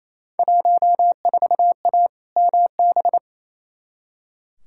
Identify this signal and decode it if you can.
Morse code, heard as a keyed tone, about 28 wpm, 700 Hz. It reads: 14A MB
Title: Field Day Exchanges Encore - 28wpm